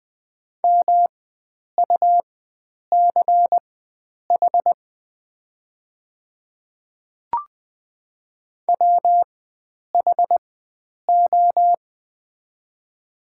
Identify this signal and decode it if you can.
Morse code, heard as a keyed tone, about 20 wpm, spaced as for 9 wpm, 700 Hz. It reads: MUCH WHO
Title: 4x Character Spacing - Top 300 Words - 20wpm